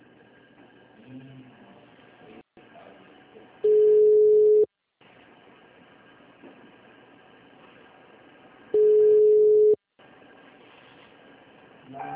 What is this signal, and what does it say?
Ringback tone (Europe, 425 Hz).